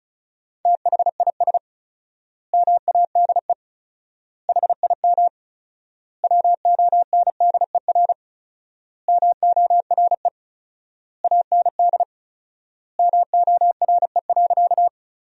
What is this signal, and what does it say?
Morse code, heard as a keyed tone, about 36 wpm, 700 Hz. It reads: THIS MADE HIM WONDER MORE AND MORE.
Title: 4x Word-Spacing - Sentences from Top 500 Words - 35wpm